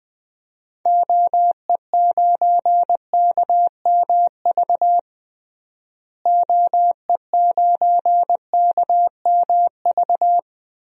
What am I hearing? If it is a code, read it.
Morse code, heard as a keyed tone, about 20 wpm, 700 Hz. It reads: OE9KMV OE9KMV
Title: DXCC-100 Call Signs - 6 Characters - Repeated 3x - 20wpm